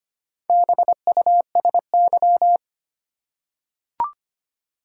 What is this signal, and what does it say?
Morse code, heard as a keyed tone, about 25 wpm, 700 Hz. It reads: BUSY
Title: Four Letter Words - 25wpm